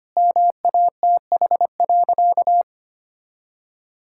Morse code, heard as a keyed tone, about 25 wpm, 700 Hz. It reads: MATH.
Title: Sentences from Top 1000 Words - Review Part 6 - 25wpm